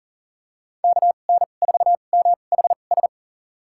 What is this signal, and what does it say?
Morse code, heard as a keyed tone, about 38 wpm, spaced as for 28 wpm, 700 Hz. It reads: KN4MHS
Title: USA Call Signs - 2x Character Spacing - 40wpm